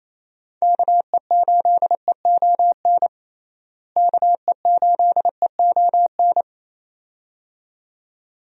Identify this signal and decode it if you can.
Morse code, heard as a keyed tone, about 28 wpm, 700 Hz. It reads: KE8EON KE8EON
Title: USA Call Signs - 6 Characters - Repeated 2x - 28wpm